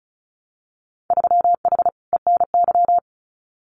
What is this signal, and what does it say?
Morse code, heard as a keyed tone, about 34 wpm, 700 Hz. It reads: 3H ENY
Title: Winter Field Day Exchanges Encore 35wpm